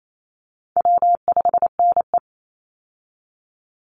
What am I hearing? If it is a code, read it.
Morse code, heard as a keyed tone, about 28 wpm, 700 Hz. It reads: W5NE
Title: Call Signs - 28wpm